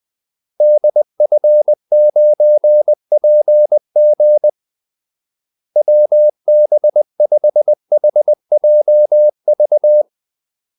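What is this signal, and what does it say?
Morse code, heard as a keyed tone, about 20 wpm, 600 Hz. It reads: DF9PG WB5HJV
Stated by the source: Morse call-sign practice at 20 wpm.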